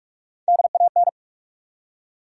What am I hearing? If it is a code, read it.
Morse code, heard as a keyed tone, about 43 wpm, 700 Hz. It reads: DAN
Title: USA Common Names - Max 3 Letters - 45wpm